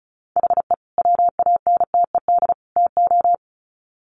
Morse code, heard as a keyed tone, about 34 wpm, 700 Hz. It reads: HE WANTED TO
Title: Sets of 3 Words 35wpm